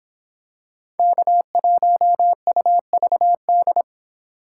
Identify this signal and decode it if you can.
Morse code, heard as a keyed tone, about 26 wpm, 700 Hz. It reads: K1UVD